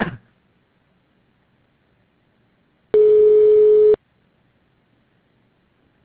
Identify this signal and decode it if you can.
Ringback tone (Europe, 425 Hz).